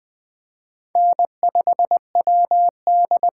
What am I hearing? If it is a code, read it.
Morse code, heard as a keyed tone, about 20 wpm, 700 Hz. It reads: N5WD
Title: USA Call Signs - 5 Characters - 6x Character Spacing - 20wpm